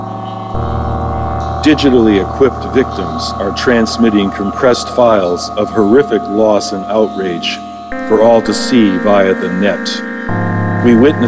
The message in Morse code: none

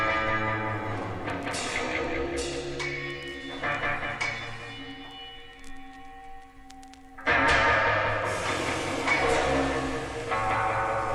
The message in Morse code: none

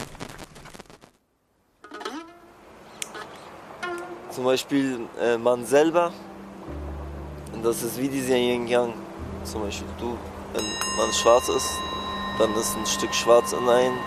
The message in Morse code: none

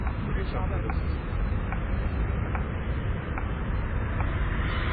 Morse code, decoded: none

